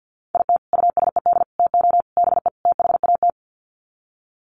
Morse code, heard as a keyed tone, about 50 wpm, 700 Hz. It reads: IT USED TO BE THAT
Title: Sets of 5 Words Encore2 50wpm